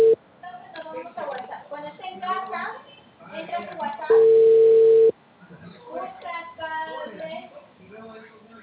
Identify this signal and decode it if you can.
Ringback tone (China, 450 Hz), with speech.